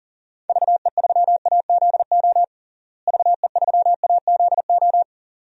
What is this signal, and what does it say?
Morse code, heard as a keyed tone, about 40 wpm, 700 Hz. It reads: VE3AZO VE3AZO
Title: DXCC-100 Call Signs - 6 Characters - Repeated 2x - 40wpm